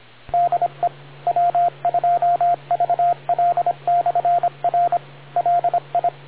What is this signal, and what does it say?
Morse code, heard as a keyed tone, about 26 wpm, 700 Hz, with radio noise. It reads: DE W2VL/R LI